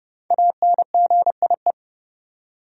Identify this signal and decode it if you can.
Morse code, heard as a keyed tone, about 31 wpm, 700 Hz. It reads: ANGIE
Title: USA Common Names - Max 5 Letters - 30wpm